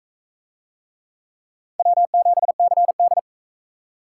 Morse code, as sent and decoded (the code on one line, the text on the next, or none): .-- --... -.-. -..
W7CD